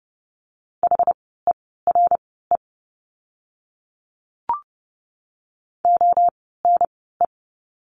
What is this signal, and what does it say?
Morse code, heard as a keyed tone, about 30 wpm, spaced as for 17 wpm, 700 Hz. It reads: HERE ONE
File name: course / 30:10wpm / 230 Lesson 37f - Letter x - Words 30wpm